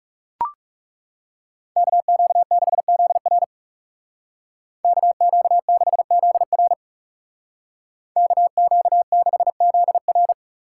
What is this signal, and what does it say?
Morse code, heard as a keyed tone, about 40 wpm, 700 Hz. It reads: KQ6ZR KQ6ZR KQ6ZR